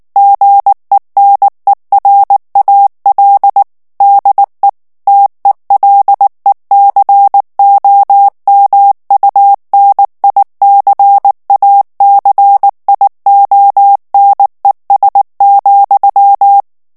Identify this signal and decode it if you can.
Morse code, heard as a keyed tone, about 19 wpm, 800 Hz. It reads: GENERAL DE TELECOMUNICACIONES,
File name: morse mp3 / ESPAGNOL TEXTE EN CLAIR 2 VITESSE 18 MOTS MINUTES